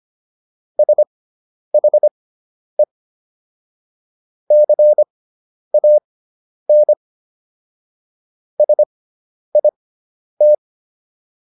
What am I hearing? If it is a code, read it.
Morse code, heard as a keyed tone, about 25 wpm, spaced as for 10 wpm, 600 Hz. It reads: SHE CAN SIT